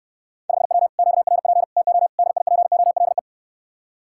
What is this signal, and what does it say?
Morse code, heard as a keyed tone, about 69 wpm, 700 Hz. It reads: 5W QRP TO DIPOLE